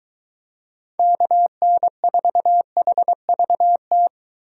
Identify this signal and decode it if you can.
Morse code, heard as a keyed tone, about 22 wpm, 700 Hz. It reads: KN4HVT